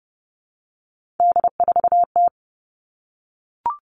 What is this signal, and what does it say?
Morse code, heard as a keyed tone, about 30 wpm, 700 Hz. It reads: D4T
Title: All Cut Numbers - 3-Digit 30wpm